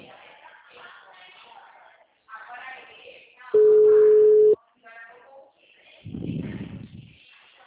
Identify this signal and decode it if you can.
Ringback tone (Europe, 425 Hz), with speech.
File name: out- -2005- - - .38